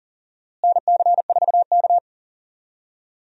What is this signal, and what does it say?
Morse code, heard as a keyed tone, about 40 wpm, 700 Hz. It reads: NC4K
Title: USA Call Signs - 4 Characters - Repeated 2x - 40wpm